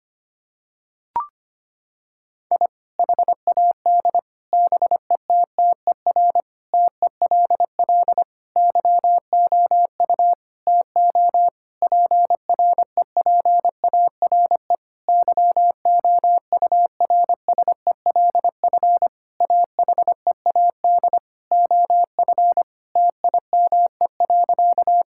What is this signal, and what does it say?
Morse code, heard as a keyed tone, about 25 wpm, 700 Hz. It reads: I HAD BETTER TELL YOU TO PREPARE YOURSELF AHEAD OF TIME.
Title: Sentences from Top 1000 Words - Review Part 6 - 25wpm